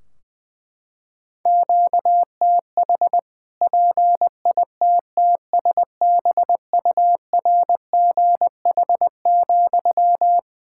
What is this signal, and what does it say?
Morse code, heard as a keyed tone, about 21 wpm, 700 Hz. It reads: QTH PITTSBURGH,
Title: Practice QSO - Exchange 1 - Beginning - 20wpm